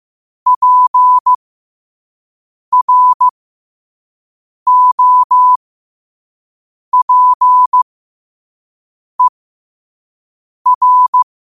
Morse code, heard as a keyed tone, about 15 wpm, spaced as for 5 wpm, 1000 Hz. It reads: PROPER
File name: qotdfiles / 5wpm